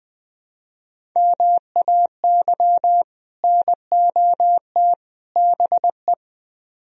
Morse code, heard as a keyed tone, about 20 wpm, 700 Hz. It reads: MAY NOT BE
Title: Speed-Racing - Sets of 3 Words 17wpm